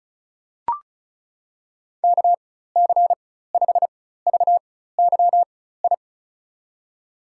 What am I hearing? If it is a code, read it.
Morse code, heard as a keyed tone, about 34 wpm, spaced as for 16 wpm, 700 Hz. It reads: KC5VYI
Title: USA Call Signs - 6 Characters - 4x Character Spacing - 35wpm